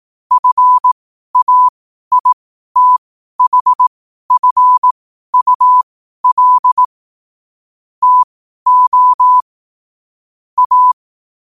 Morse code, heard as a keyed tone, about 18 wpm, spaced as for 12 wpm, 1000 Hz. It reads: FAITHFUL TO A
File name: qotdfiles / 12wpm